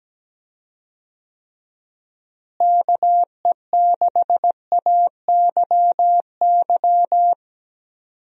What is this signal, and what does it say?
Morse code, heard as a keyed tone, about 17 wpm, 700 Hz. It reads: KE6AYY
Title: USA Call Signs - 6 Characters - Repeated 2x - 17wpm